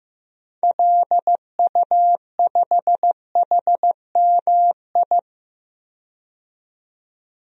Morse code, heard as a keyed tone, about 16 wpm, 700 Hz. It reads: LU5HMI
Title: DXCC-100 Call Signs - 6 Characters - 15wpm